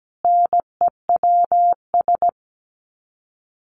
Morse code, heard as a keyed tone, about 17 wpm, 700 Hz. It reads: NEWS
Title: Sets of 5 Words Encore 17wpm